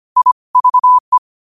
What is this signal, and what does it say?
Morse code, heard as a keyed tone, about 25 wpm, 1000 Hz. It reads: I VE
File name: qotdfiles / 25wpm